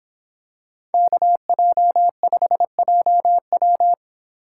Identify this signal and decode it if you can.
Morse code, heard as a keyed tone, about 27 wpm, 700 Hz. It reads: KJ5JW
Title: Speed Racing - USA 2x2 Call Signs 22wpm